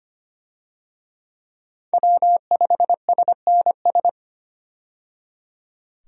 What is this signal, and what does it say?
Morse code, heard as a keyed tone, about 25 wpm, 700 Hz. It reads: W5SNS
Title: USA Call Signs - 5 Characters - 25wpm